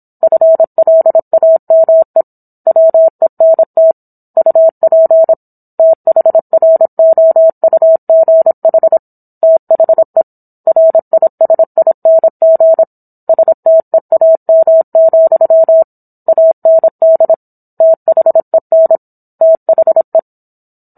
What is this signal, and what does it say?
Morse code, heard as a keyed tone, about 26 wpm, 650 Hz. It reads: FLAME WENT UP THROUGH THE RISING STEAM, AND THEN THE